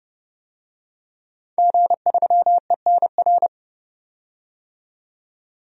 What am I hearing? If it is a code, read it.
Morse code, heard as a keyed tone, about 30 wpm, 700 Hz. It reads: G3ENR
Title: DXCC-100 Call Signs - 5 Characters - 30wpm